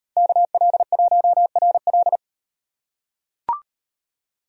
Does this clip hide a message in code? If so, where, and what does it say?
Yes: Morse code from 0.2 to 2.2 s, heard as a keyed tone, about 35 wpm, 700 Hz. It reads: KL1RL